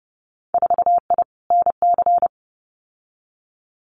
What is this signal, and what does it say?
Morse code, heard as a keyed tone, about 30 wpm, 700 Hz. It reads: 4I NC
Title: Winter Field Day Exchanges 30wpm